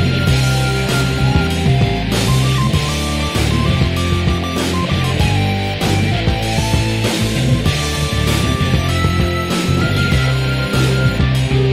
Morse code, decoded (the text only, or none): none